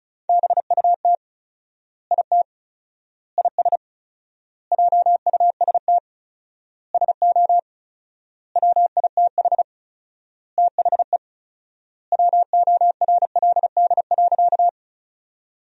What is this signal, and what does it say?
Morse code, heard as a keyed tone, about 34 wpm, 700 Hz. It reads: BUT IT IS JUST SO WITH THE WORLD.
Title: 4x Word-Spacing - Sentences from Top 200 Words - 35wpm